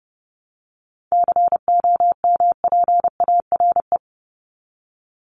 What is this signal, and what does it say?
Morse code, heard as a keyed tone, about 30 wpm, 700 Hz. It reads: COMPARE
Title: Top 1251-1500 Words 30wpm